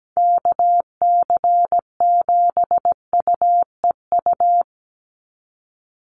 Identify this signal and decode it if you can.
Morse code, heard as a keyed tone, about 17 wpm, 700 Hz. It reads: KC7UEU
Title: USA Call Signs 17wpm